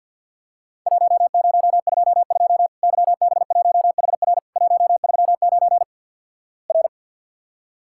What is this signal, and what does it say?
Morse code, heard as a keyed tone, about 48 wpm, 700 Hz. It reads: 1022 YB1HR 139 R
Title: Worked All Europe DX Contest - QTC Exchange with Repeats - 50wpm